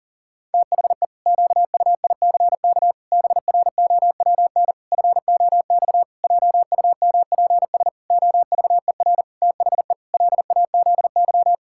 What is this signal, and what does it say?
Morse code, heard as a keyed tone, about 37 wpm, 700 Hz. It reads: THE QUICK BROWN FOX JUMPS OVER THE LAZY
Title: Pangram Challenge v2 - 40wpm